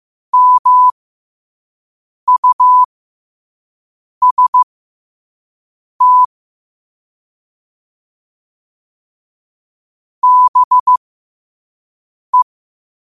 Morse code, heard as a keyed tone, about 15 wpm, spaced as for 5 wpm, 1000 Hz. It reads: MUST BE
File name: qotdfiles / 5wpm